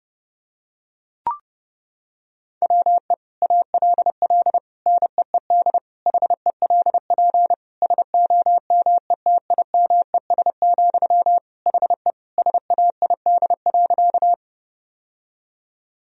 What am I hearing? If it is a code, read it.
Morse code, heard as a keyed tone, about 30 wpm, 700 Hz. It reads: WE ALL NEED HELP SOMETIMES, HE SAID.
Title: Sentences from Top 1000 Words - Review Part 3 - 30wpm